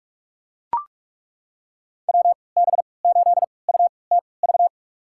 Morse code, heard as a keyed tone, about 42 wpm, spaced as for 25 wpm, 700 Hz. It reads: WB8UTV
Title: USA Call Signs - 6 Characters - 3x Character Spacing - 45wpm